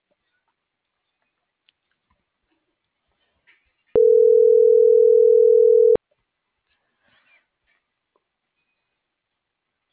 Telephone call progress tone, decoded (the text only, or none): ringback tone